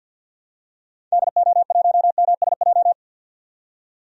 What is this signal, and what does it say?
Morse code, heard as a keyed tone, about 46 wpm, 700 Hz. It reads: DO1MSJ
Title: DXCC-100 Call Signs - 6 Characters - 50wpm